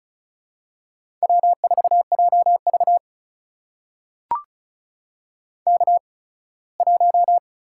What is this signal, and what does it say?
Morse code, heard as a keyed tone, about 36 wpm, spaced as for 21 wpm, 700 Hz. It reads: W4JV K1
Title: USA Call Signs - 4 Characters - 8x Character Spacing - 35wpm